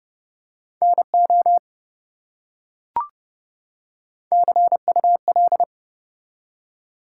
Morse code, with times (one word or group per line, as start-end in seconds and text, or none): 0.8-1.6 NO
4.3-5.6 CUL